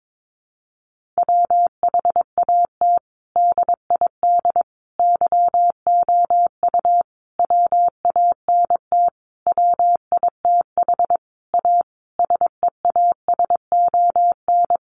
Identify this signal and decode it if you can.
Morse code, heard as a keyed tone, about 22 wpm, 700 Hz. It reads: WHAT DID YOU WANT WITH A SEASON
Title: Sentences from Top 1400 Words 22wpm